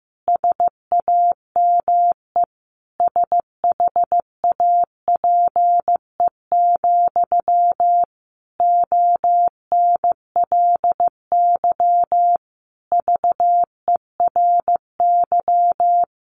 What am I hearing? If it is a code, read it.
Morse code, heard as a keyed tone, about 15 wpm, 700 Hz. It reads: SAME SHAPE, ONLY VERY
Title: Sentences from Top 1400 Words 15wpm